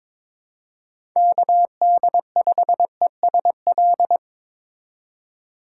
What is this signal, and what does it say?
Morse code, heard as a keyed tone, about 21 wpm, 700 Hz. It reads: KD5ESL
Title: USA Call Signs - 6 Characters - 3x Character Spacing - 22wpm